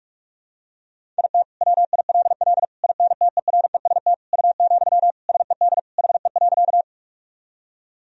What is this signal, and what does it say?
Morse code, heard as a keyed tone, about 43 wpm, 700 Hz. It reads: IT WILL INTEREST U, SED HE.
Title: Sentences from Top 400 Words and Abbreviations 45wpm